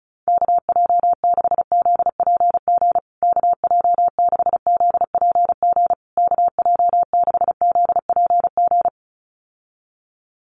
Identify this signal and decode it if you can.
Morse code, heard as a keyed tone, about 34 wpm, 700 Hz. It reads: KJ6ZPG KJ6ZPG KJ6ZPG
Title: USA Call Signs Repeated 3x 35wpm